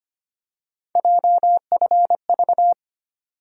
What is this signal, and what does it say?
Morse code, heard as a keyed tone, about 25 wpm, 700 Hz. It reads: JFV
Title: USA Call Sign Postfixes - 3 Letters 25wpm